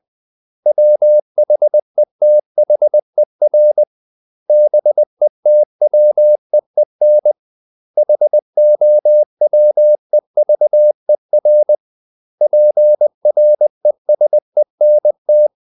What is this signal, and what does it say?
Morse code, heard as a keyed tone, about 20 wpm, 600 Hz. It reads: WHETHER BETWEEN HOWEVER PRESENT
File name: CW / Practice / 500MostCommonEnglishWords20